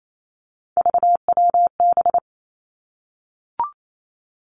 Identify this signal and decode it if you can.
Morse code, heard as a keyed tone, about 28 wpm, 700 Hz. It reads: VWB
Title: Three Challenging Letter-Number Combinations 28wpm